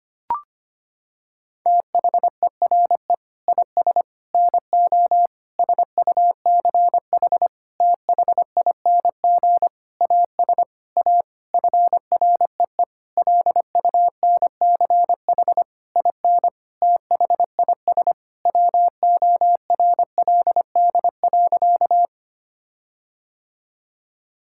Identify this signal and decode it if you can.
Morse code, heard as a keyed tone, about 25 wpm, 700 Hz. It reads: THERE IS NO SUCH THING AS A FREE LUNCH IN THIS WORLD.
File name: practice / 25wpm / Sentences with 9-Word Phrases - 25wpm